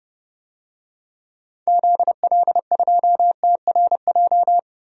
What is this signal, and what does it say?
Morse code, heard as a keyed tone, about 32 wpm, 700 Hz. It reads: ZL2TRJ
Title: DXCC-100 Call Signs - 6 Characters - Repeated 2x - 30wpm